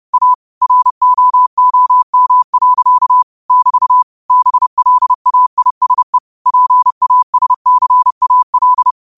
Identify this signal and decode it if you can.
Morse code, heard as a keyed tone, about 30 wpm, 1000 Hz. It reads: A ROOM. <BT> BLAISE PASCAL
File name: qotdfiles / 30wpm